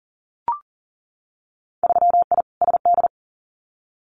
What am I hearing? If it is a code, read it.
Morse code, heard as a keyed tone, about 40 wpm, 700 Hz. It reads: 3I SD
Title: Winter Field Day Exchanges Encore 40wpm